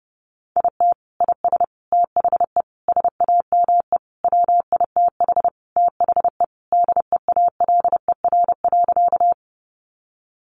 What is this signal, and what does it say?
Morse code, heard as a keyed tone, about 30 wpm, 700 Hz. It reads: IT IS THE SAME WITH THE DEALER.
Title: Sentences from Top 1100 Words 30wpm